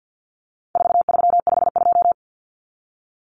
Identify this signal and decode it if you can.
Morse code, heard as a keyed tone, about 50 wpm, 700 Hz. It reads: 4352
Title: Common Cut Numbers - 4-Digit 50wpm